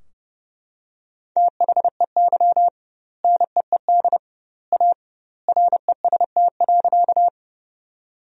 Morse code, heard as a keyed tone, about 30 wpm, 700 Hz. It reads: THEY NEED A REST.
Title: 2x Word-Spacing - Sentences from Top 500 Words - 30wpm